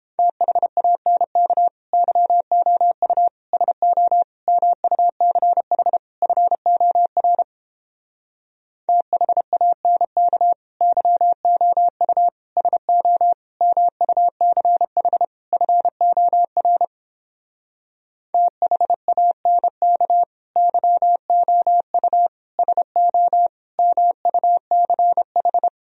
Morse code, heard as a keyed tone, about 29 wpm, 700 Hz. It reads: THANK YOU SO MUCH FOR THANK YOU SO MUCH FOR THANK YOU SO MUCH